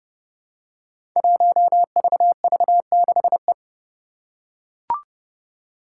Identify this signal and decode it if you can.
Morse code, heard as a keyed tone, about 30 wpm, 700 Hz. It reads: 1VV6E
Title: Five Challenging Letter-Number Combinations 30wpm